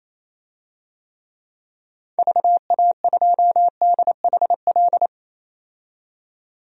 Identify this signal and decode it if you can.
Morse code, heard as a keyed tone, about 28 wpm, 700 Hz. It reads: VA2DHL
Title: DXCC-100 Call Signs - 6 Characters - 28wpm